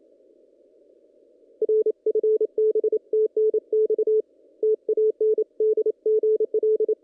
Morse code, heard as a keyed tone, about 28 wpm, 430 Hz. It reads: R FB TNX TAN DGL
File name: contests / ss-cw-k5tr-op-n5rz -1700